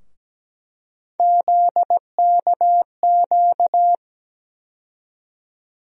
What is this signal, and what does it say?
Morse code, heard as a keyed tone, about 17 wpm, 700 Hz. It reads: ZKQ
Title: USA Call Sign Postfixes - 3 Letters 17wpm